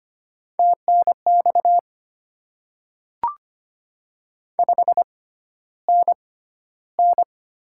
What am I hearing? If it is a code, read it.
Morse code, heard as a keyed tone, about 25 wpm, spaced as for 13 wpm, 700 Hz. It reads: TNX 5NN
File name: Top 20 QSO Elements - 6x Character Spacing - 25wpm